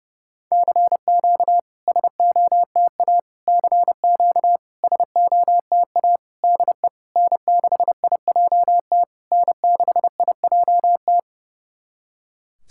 Morse code, heard as a keyed tone, about 30 wpm, 700 Hz. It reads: CQ SOTA CQ SOTA DE N6IJT N6IJT
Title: SOTA QSO Practice 30wpm